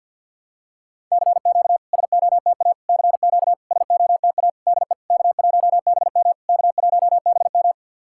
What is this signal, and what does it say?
Morse code, heard as a keyed tone, about 50 wpm, 700 Hz. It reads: CQ SOTA CQ SOTA DE K1BM K1BM
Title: SOTA QSO Practice 50wpm